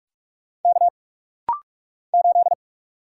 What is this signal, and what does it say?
Morse code, heard as a keyed tone, about 46 wpm, 700 Hz. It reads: K 8
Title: Single Letter-Number - Rapid-Fire 45wpm